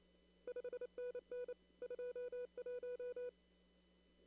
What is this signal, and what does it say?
Morse code, heard as a keyed tone, about 28 wpm, 490 Hz. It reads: 5NN 21
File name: CW / kt5j wpx cw-op-k5pi